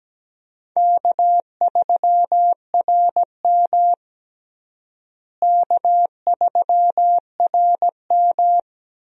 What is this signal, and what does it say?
Morse code, heard as a keyed tone, about 17 wpm, 700 Hz. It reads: K3RM K3RM
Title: USA Call Signs - 4 Characters - Repeated 2x - 17wpm